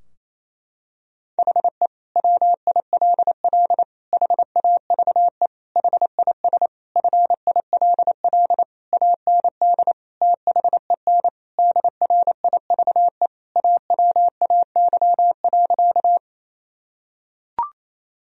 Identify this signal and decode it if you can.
Morse code, heard as a keyed tone, about 28 wpm, 700 Hz. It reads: HE WILL HAVE HIS FILL AND THEN DRIVE AWAY.
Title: Sentences from Top 1000 Words - Review Part 3 - 28wpm